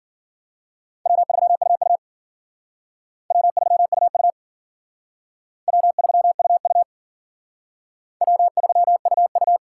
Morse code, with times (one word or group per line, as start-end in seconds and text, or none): 1.1-2.0 W3UU
3.3-4.3 W3UU
5.7-6.8 W3UU
8.2-9.6 W3UU